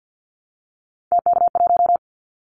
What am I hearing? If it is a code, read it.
Morse code, heard as a keyed tone, about 49 wpm, 700 Hz. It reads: TK1
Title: Callsign Prefixes for Top 100 DXCC - 50wpm